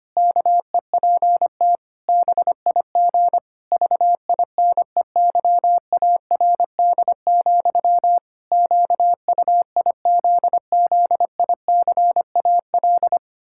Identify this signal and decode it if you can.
Morse code, heard as a keyed tone, about 25 wpm, 700 Hz. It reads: KEPT BIG VINEYARD, QUIZZICAL
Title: Pangram Challenge v2 - 25wpm